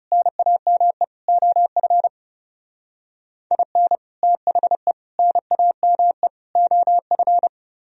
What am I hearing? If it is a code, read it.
Morse code, heard as a keyed tone, about 37 wpm, 700 Hz. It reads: NAME OF IN THE NAME OF
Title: Speed-Racing - Sets of 4 Words 30wpm